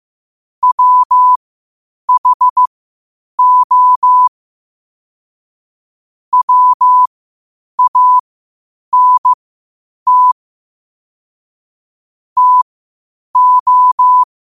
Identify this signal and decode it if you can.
Morse code, heard as a keyed tone, about 15 wpm, spaced as for 8 wpm, 1000 Hz. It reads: WHO WANT TO
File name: qotdfiles / 8wpm